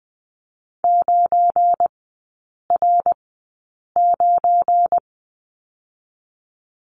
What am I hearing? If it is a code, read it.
Morse code, heard as a keyed tone, about 20 wpm, spaced as for 8 wpm, 700 Hz. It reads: 9R9
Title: 184 Lesson 30e - Letter b - Three Characters 20wpm